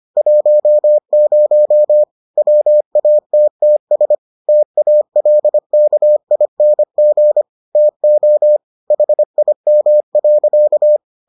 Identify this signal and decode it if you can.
Morse code, heard as a keyed tone, about 25 wpm, 600 Hz. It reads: 10 WATTS TALKING TO HIM.